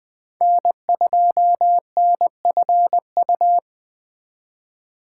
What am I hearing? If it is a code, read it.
Morse code, heard as a keyed tone, about 21 wpm, 700 Hz. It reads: N2NFU